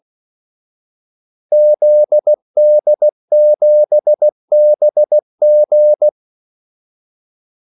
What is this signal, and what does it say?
Morse code, heard as a keyed tone, about 16 wpm, 600 Hz. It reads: ZD7BG
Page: Kaldesignaler 16 wpm